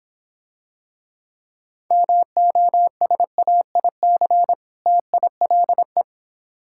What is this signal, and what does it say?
Morse code, heard as a keyed tone, about 26 wpm, 700 Hz. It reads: MOSAIC TILE